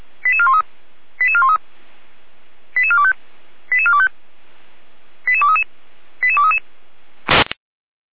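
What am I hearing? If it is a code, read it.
Five-tone selective call (ZVEI), heard as quick runs of five tones, tones of about 70 ms. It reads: 88413 88413 88425 88425 88233 88233
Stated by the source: ZVEI five-tone call 88251 received by SDR.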